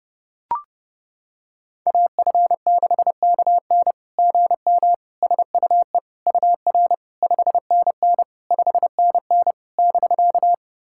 Morse code, heard as a keyed tone, about 30 wpm, 690 Hz. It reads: AF6KN GM SUE UR 5NN 5NN <BK>
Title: SOTA QSO Practice 30wpm